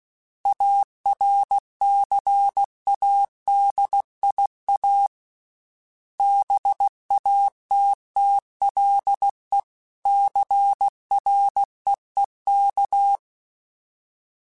Morse code, heard as a keyed tone, about 16 wpm, 800 Hz. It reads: ARCADIA BATTLE CREEK
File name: fromSMs / FromSMs mp3s / Cities@15